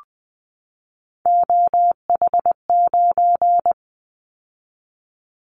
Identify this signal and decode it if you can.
Morse code, heard as a keyed tone, about 20 wpm, 700 Hz. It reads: OH9
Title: Callsign Prefixes for Top 50 DXCC - 20wpm